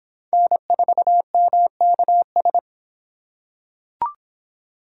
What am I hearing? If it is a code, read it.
Morse code, heard as a keyed tone, about 26 wpm, 700 Hz. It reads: N4MKS